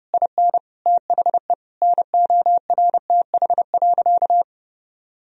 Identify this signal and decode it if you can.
Morse code, heard as a keyed tone, about 30 wpm, 700 Hz. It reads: IN THE NORTH.
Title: Sentences from Top 1000 Words - Review Part 1 - 30wpm